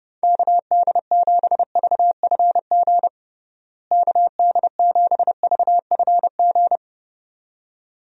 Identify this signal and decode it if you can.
Morse code, heard as a keyed tone, about 32 wpm, 700 Hz. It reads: KD7VFG KD7VFG
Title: USA Call Signs - 6 Characters - Repeated 2x - 30wpm